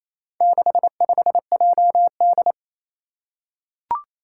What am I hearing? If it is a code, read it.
Morse code, heard as a keyed tone, about 28 wpm, 700 Hz. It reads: 65JD
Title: Four Challenging Letter-Number Combinations 28wpm